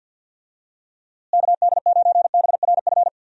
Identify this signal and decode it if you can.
Morse code, heard as a keyed tone, about 50 wpm, 700 Hz. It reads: KD9BRF
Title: USA Call Signs - 6 Characters - Repeated 2x - 50wpm